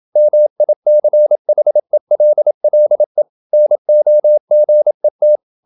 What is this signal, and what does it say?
Morse code, heard as a keyed tone, about 27 wpm, 600 Hz. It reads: MICHELLE NOGET
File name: Ferie QRQ 27wpm